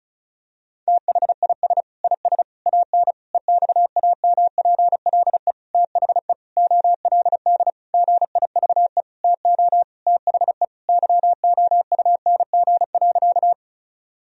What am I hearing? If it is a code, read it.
Morse code, heard as a keyed tone, about 34 wpm, 700 Hz. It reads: THIS IS AN EXAMPLE THE OLD GIVE TO THE YOUNG.
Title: Sentences from Top 1000 Words - Review Part 1 - 35wpm